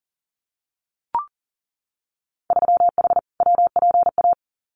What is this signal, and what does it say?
Morse code, heard as a keyed tone, about 38 wpm, 700 Hz. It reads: 3H WPA
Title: Winter Field Day Exchanges 40wpm